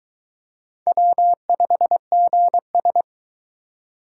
Morse code, heard as a keyed tone, about 23 wpm, 700 Hz. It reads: W5GS